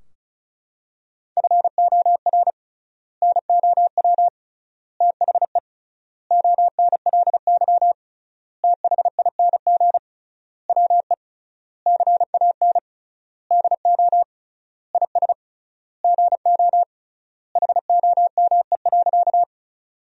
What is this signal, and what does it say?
Morse code, heard as a keyed tone, about 36 wpm, 700 Hz. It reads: FOR NOW THE ONLY THING WE CAN DO IS GO HOME.
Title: 3x Word-Spacing - Sentences from Top 200 Words - 35wpm